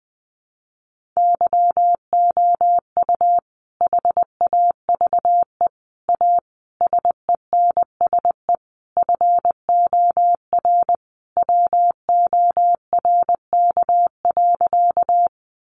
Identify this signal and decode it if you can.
Morse code, heard as a keyed tone, about 20 wpm, 700 Hz. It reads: YOU HAVE A SENSE FOR WORK.
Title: Sentences from Top 700 Words 20wpm